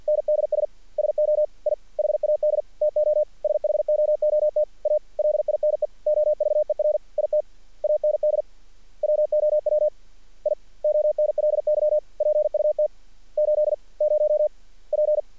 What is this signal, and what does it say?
Morse code, heard as a keyed tone, about 48 wpm, 590 Hz. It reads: NDS SO I HAD TO SHOOT A LINE OVER IT AND WOW I ONLY PUT 80 P
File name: archived / CRSnet